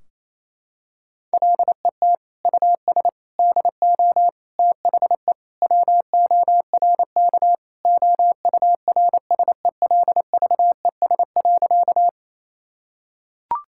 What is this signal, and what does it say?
Morse code, heard as a keyed tone, about 28 wpm, 700 Hz. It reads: LET US DO THE WORK OURSELVES.
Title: Sentences from Top 1600 Words 28wpm